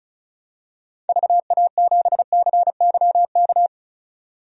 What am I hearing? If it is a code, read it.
Morse code, heard as a keyed tone, about 36 wpm, 700 Hz. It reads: VA7CYK
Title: DXCC-100 Call Signs - 6 Characters - 35wpm